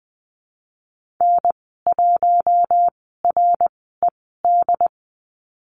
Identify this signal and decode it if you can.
Morse code, heard as a keyed tone, about 20 wpm, spaced as for 15 wpm, 700 Hz. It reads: N1RED
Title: 069 Lesson 12g - Letter d - Call Signs 20wpm